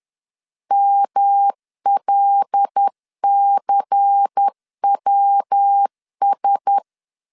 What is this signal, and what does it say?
Morse code, heard as a keyed tone, about 10 wpm, 800 Hz. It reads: MLCWS